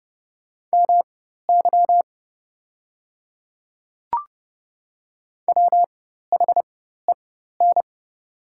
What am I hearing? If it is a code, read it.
Morse code, heard as a keyed tone, about 30 wpm, spaced as for 14 wpm, 690 Hz. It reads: MY WHEN
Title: 4x Character Spacing - Top 200 Words - 30wpm